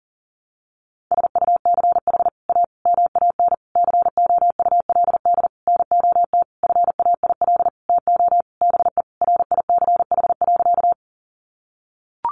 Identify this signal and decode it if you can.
Morse code, heard as a keyed tone, about 40 wpm, 700 Hz. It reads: SUCH A MAN COULD NOT FAIL TO BE RICH.
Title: Sentences from Top 1100 Words 40wpm